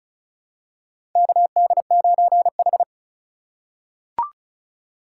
Morse code, heard as a keyed tone, about 36 wpm, 700 Hz. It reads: KD9H
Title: USA Call Signs - 4 Characters - 4x Character Spacing - 35wpm